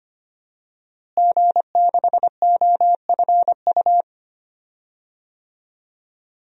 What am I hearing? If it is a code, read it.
Morse code, heard as a keyed tone, about 25 wpm, 700 Hz. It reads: G6OFU
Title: DXCC-100 Call Signs - 5 Characters - Repeated 2x - 25wpm